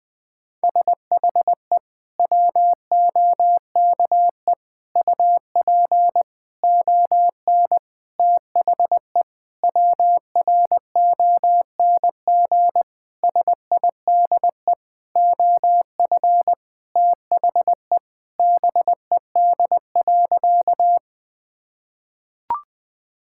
Morse code, heard as a keyed tone, about 20 wpm, 700 Hz. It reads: SHE WOKE UP ON THE WRONG SIDE OF THE BED.
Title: Sentences with 6-Word Phrases - Spoken Beginning - 20wpm